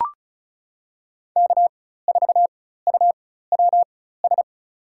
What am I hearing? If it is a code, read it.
Morse code, heard as a keyed tone, about 36 wpm, spaced as for 16 wpm, 700 Hz. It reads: K4UWS
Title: USA Call Signs - 5 Characters - 4x Character Spacing - 35wpm